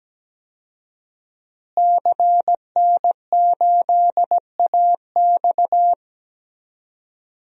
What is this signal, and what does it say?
Morse code, heard as a keyed tone, about 17 wpm, 700 Hz. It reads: CN8AX
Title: DXCC-100 Call Signs - 5 Characters - Repeated 2x - 17wpm